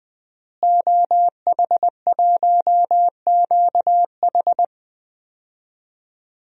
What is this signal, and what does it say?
Morse code, heard as a keyed tone, about 20 wpm, 700 Hz. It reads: OH1QH
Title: DXCC-100 Call Signs - 5 Characters - Repeated 3x - 20wpm